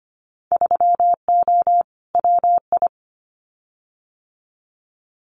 Morse code, heard as a keyed tone, about 25 wpm, 700 Hz. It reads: 3O WI
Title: Winter Field Day Exchanges 25wpm